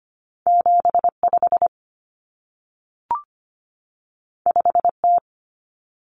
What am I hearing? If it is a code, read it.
Morse code, heard as a keyed tone, about 25 wpm, 700 Hz. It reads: 75 5T
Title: Common Cut Numbers - 2-Digit 25wpm